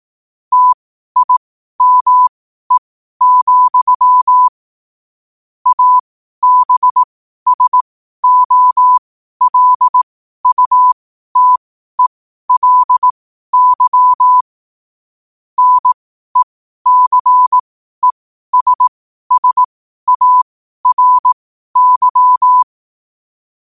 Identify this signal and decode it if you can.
Morse code, heard as a keyed tone, about 18 wpm, spaced as for 12 wpm, 1000 Hz. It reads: TIME, ABSOLUTELY NECESSARY
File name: qotdfiles / 12wpm